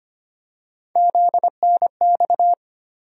Morse code, heard as a keyed tone, about 25 wpm, 700 Hz. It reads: ZNX
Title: USA Call Sign Postfixes - 3 Letters 25wpm